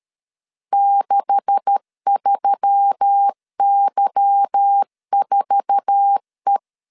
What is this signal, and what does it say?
Morse code, heard as a keyed tone, about 13 wpm, 800 Hz. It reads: 63Y4E
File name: CW / DET / 1 9 1500C S A 12wpm 3800